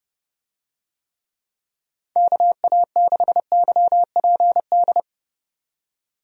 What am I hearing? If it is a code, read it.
Morse code, heard as a keyed tone, about 31 wpm, 700 Hz. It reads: KA6YPD
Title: USA Call Signs - 6 Characters - 30wpm